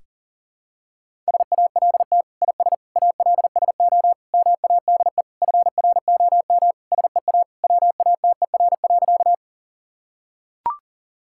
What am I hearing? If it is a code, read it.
Morse code, heard as a keyed tone, about 40 wpm, 700 Hz. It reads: SALT IS ALSO MADE FROM SEA WATER.
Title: Sentences from Top 1000 Words - Review Part 5 - 40wpm